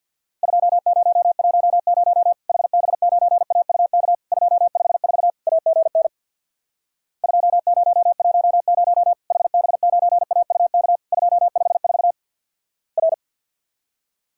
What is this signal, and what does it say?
Morse code, heard as a keyed tone, about 50 wpm, 700 Hz. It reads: 2010 HB9AUK 254 AGN 2010 HB9AUK 254 R
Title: Worked All Europe DX Contest - QTC Exchange with Repeats - 50wpm